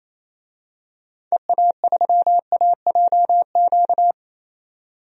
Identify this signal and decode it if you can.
Morse code, heard as a keyed tone, about 28 wpm, 700 Hz. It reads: EA3AJQ
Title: DXCC-100 Call Signs - 6 Characters - 28wpm